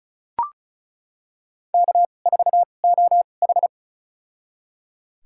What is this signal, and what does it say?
Morse code, heard as a keyed tone, about 34 wpm, spaced as for 25 wpm, 700 Hz. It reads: K4OH
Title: USA Call Signs - 4 Characters - 2x Character Spacing - 35wpm